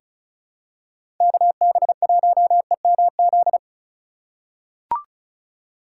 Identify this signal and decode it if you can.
Morse code, heard as a keyed tone, about 36 wpm, 700 Hz. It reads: KB1EMZ
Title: USA Call Signs - 6 Characters - 4x Character Spacing - 35wpm